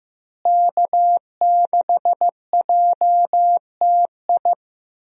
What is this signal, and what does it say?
Morse code, heard as a keyed tone, about 15 wpm, 700 Hz. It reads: K6JTI
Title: USA Call Signs - 5 Characters - Repeated 3x - 15wpm